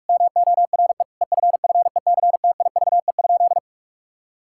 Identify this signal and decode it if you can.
Morse code, heard as a keyed tone, about 41 wpm, 700 Hz. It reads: MORE EFFECTIVE?
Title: Sentences from Top 1000 Words - Review Part 6 - 45wpm